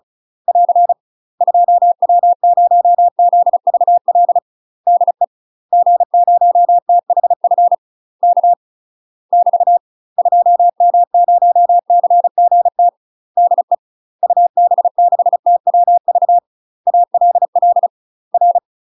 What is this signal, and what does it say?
Morse code, heard as a keyed tone, about 34 wpm, 700 Hz. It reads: <AR> 2W0ZVL DE G0THF K <BT> 2M0CGT DE UB6TWV ALL R
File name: UK MASTER QSOs TRAINING / 35WPM / UK MASTER QSOs 0041